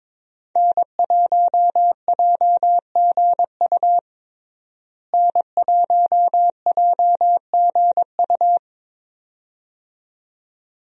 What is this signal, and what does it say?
Morse code, heard as a keyed tone, about 22 wpm, 700 Hz. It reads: N1JGU N1JGU
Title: USA Call Signs - 5 Characters - Repeated 2x - 22wpm